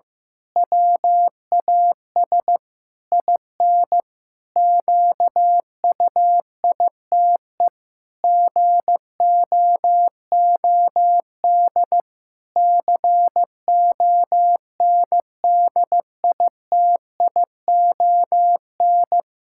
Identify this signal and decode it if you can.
Morse code, heard as a keyed tone, about 16 wpm, 700 Hz. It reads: WAS IN QUITE GOOD CONDITION
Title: Pangram Challenge v2 - 15wpm